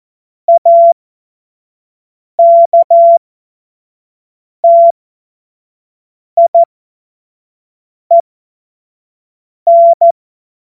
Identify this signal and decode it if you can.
Morse code, heard as a keyed tone, about 14 wpm, spaced as for 5 wpm, 680 Hz. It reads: AKTIEN
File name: Witz-29-Juli-2025-25-F70